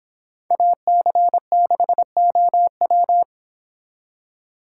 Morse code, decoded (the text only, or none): AC6OW